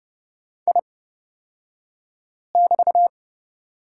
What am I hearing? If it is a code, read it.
Morse code, heard as a keyed tone, about 29 wpm, spaced as for 5 wpm, 700 Hz. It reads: I<BT>
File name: Lesson 36d - BT - Two Characters 30wpm